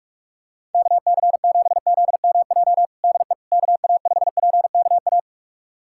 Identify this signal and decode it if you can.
Morse code, heard as a keyed tone, about 45 wpm, 700 Hz. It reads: KC7ZMJ DE KA5PKA
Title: USA Callsign DE Callsign - 6 Characters - 45wpm